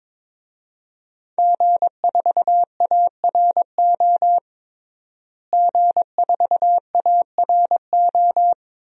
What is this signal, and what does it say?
Morse code, heard as a keyed tone, about 21 wpm, 700 Hz. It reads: G4ARO G4ARO
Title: DXCC-100 Call Signs - 5 Characters - Repeated 2x - 22wpm